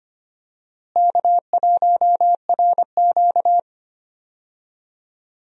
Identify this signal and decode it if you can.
Morse code, heard as a keyed tone, about 25 wpm, 700 Hz. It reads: K1RQ
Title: USA Call Signs - 4 Characters - Repeated 2x - 25wpm